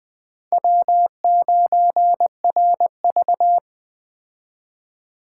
Morse code, heard as a keyed tone, about 21 wpm, 700 Hz. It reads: W9RV